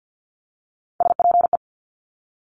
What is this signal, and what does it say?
Morse code, heard as a keyed tone, about 50 wpm, 700 Hz. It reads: SPE
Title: Top Three Letter Combinations in Words 50wpm